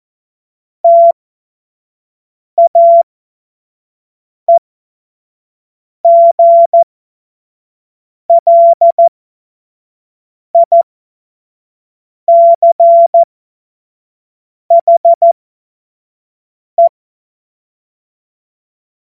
Morse code, heard as a keyed tone, about 14 wpm, spaced as for 5 wpm, 680 Hz. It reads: TAEGLICHE
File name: Witz-29-Juli-2025-25-F70